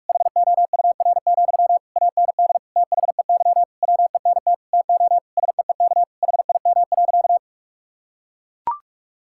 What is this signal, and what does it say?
Morse code, heard as a keyed tone, about 42 wpm, 700 Hz. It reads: HOUR, AND THEY WENT TO SEEK HIM.
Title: Sentences from Top 1600 Words 45wpm